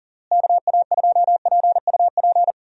Morse code, heard as a keyed tone, about 40 wpm, 700 Hz. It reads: KA2PUP
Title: USA Call Signs - 6 Characters - Repeated 3x - 40wpm